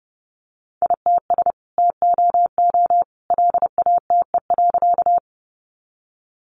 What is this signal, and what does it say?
Morse code, heard as a keyed tone, about 30 wpm, 700 Hz. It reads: ITS TOO LATE.
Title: Sentences from Top 600 Words 30wpm